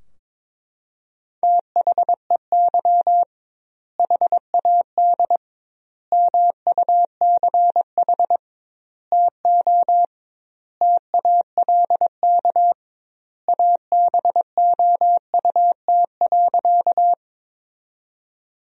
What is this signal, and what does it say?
Morse code, heard as a keyed tone, about 22 wpm, 700 Hz. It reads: THEY HAD MUCH TO TALK ABOUT.
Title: 2x Word-Spacing - Sentences from Top 400 Words - 22wpm